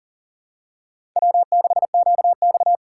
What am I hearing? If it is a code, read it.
Morse code, heard as a keyed tone, about 40 wpm, 700 Hz. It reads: W6QX
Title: USA Call Signs - 4 Characters - 2x Character Spacing - 40wpm